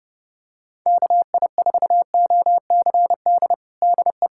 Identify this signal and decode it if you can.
Morse code, heard as a keyed tone, about 30 wpm, 690 Hz. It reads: KI4OCD DE
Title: USA Callsign DE Callsign - 6 Characters - 30wpm